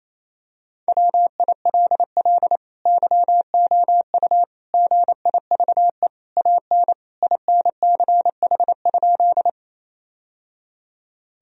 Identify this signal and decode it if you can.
Morse code, heard as a keyed tone, about 28 wpm, 700 Hz. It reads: WILL YOU GIVE AN INCH?
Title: Sentences from Top 1000 Words - Review Part 3 - 28wpm